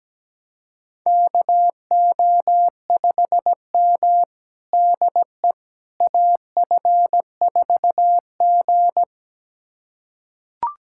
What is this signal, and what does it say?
Morse code, heard as a keyed tone, about 17 wpm, 700 Hz. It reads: KO5M DE AF4G
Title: USA Callsign DE Callsign - 4 Characters - 17wpm